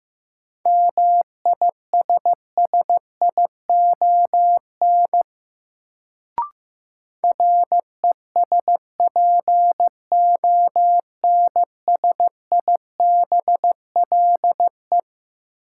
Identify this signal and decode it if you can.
Morse code, heard as a keyed tone, about 15 wpm, 700 Hz. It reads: MISSION RESPONSIBLE
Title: Sentences from Top 1600 Words - New Word Drill Alt - 15wpm